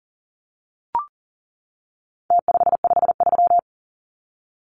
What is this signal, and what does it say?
Morse code, heard as a keyed tone, about 40 wpm, 700 Hz. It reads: T553
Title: Numbers - Contest Style - 4-Digit 40wpm